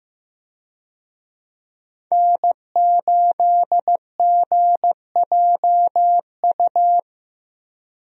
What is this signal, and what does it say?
Morse code, heard as a keyed tone, about 15 wpm, 700 Hz. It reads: N8GJU
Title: USA Call Signs - 5 Characters - Repeated 2x - 15wpm